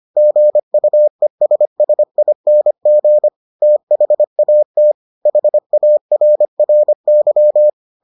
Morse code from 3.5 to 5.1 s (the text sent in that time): THAT